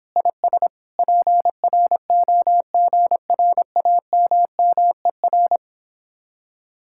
Morse code, heard as a keyed tone, about 29 wpm, 700 Hz. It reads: IS PROGRAMMER